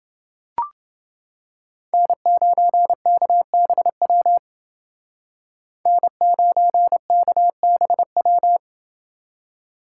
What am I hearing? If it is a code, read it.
Morse code, heard as a keyed tone, about 28 wpm, 700 Hz. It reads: N9KBW N9KBW